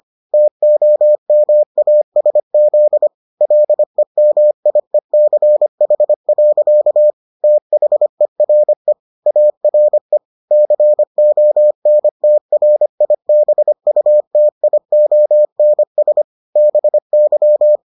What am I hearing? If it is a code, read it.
Morse code, heard as a keyed tone, about 25 wpm, 600 Hz. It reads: TOMASZ LEMIECH. THERE ARE CONTRIBUTIONS BY